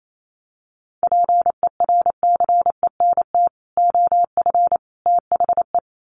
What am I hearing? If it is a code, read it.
Morse code, heard as a keyed tone, about 28 wpm, 700 Hz. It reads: PERCENT OF THE
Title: Sets of 3 Words 28wpm